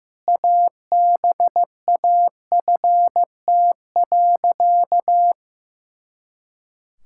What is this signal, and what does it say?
Morse code, heard as a keyed tone, about 16 wpm, 700 Hz. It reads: ABAFT.
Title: Pangram Challenge v2 - 15wpm